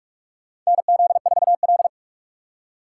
Morse code, heard as a keyed tone, about 46 wpm, 700 Hz. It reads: NZ4L
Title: USA Call Signs - 4 Characters - 4x Character Spacing - 45wpm